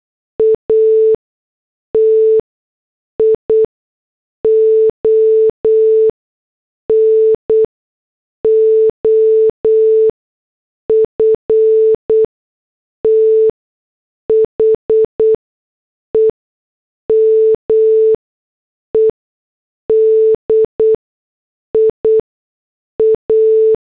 Morse code, heard as a keyed tone, about 8 wpm, spaced as for 6 wpm, 440 Hz. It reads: ATION OF THE MEDIA